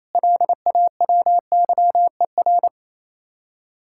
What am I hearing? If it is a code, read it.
Morse code, heard as a keyed tone, about 35 wpm, 700 Hz. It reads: LAWYER